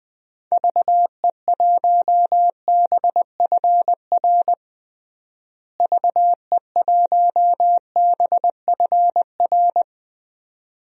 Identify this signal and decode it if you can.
Morse code, heard as a keyed tone, about 20 wpm, 700 Hz. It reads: VE1BFR VE1BFR
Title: DXCC-100 Call Signs - 6 Characters - Repeated 2x - 20wpm